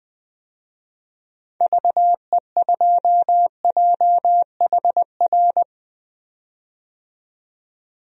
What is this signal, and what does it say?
Morse code, heard as a keyed tone, about 20 wpm, 700 Hz. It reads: VE2JHR
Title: DXCC-100 Call Signs - 6 Characters - 20wpm